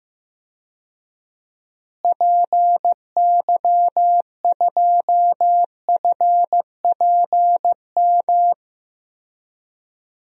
Morse code, heard as a keyed tone, about 15 wpm, 700 Hz. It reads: PY2FPM
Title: DXCC-100 Call Signs - 6 Characters - Repeated 2x - 15wpm